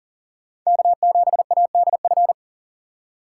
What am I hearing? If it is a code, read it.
Morse code, heard as a keyed tone, about 40 wpm, 700 Hz. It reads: K7ADF
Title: USA Call Signs - 5 Characters - 3x Character Spacing - 40wpm